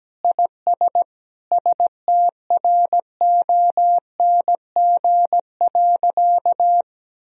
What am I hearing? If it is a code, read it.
Morse code, heard as a keyed tone, about 17 wpm, 700 Hz. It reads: IS STRONG.
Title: Sentences from Top 1000 Words - Review Part 4 - 17wpm